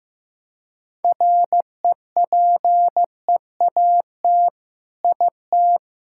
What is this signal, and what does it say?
Morse code, heard as a keyed tone, about 16 wpm, 700 Hz. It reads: REPEAT IT
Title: Sentences from Top 1600 Words 15wpm